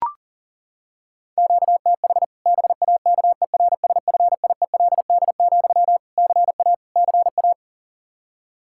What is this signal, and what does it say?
Morse code, heard as a keyed tone, about 40 wpm, 700 Hz. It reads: QTH BAKERSFIELD, CA CA
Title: Practice QSO - Exchange 1 - Beginning - 40wpm